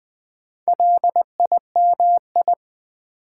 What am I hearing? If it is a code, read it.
Morse code, heard as a keyed tone, about 20 wpm, 700 Hz. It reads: LIMI
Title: Build A Word - 4 Letters At A Time - 10-Letter Words - 20wpm